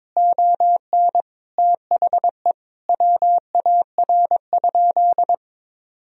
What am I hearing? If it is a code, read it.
Morse code, heard as a keyed tone, about 22 wpm, 700 Hz. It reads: ON THE WAR?
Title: Sentences from Top 1000 Words - Review Part 4 - 22wpm